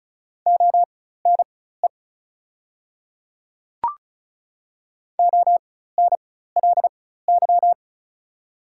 Morse code, heard as a keyed tone, about 35 wpm, spaced as for 16 wpm, 690 Hz. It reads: ONE ONLY
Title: 4x Character Spacing - Top 200 Words - 35wpm